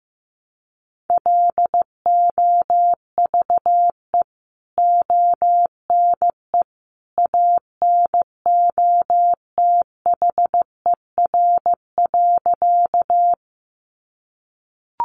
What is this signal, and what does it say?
Morse code, heard as a keyed tone, about 15 wpm, 700 Hz. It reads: LOVE ONE ANOTHER.
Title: Sentences from Top 600 Words 15wpm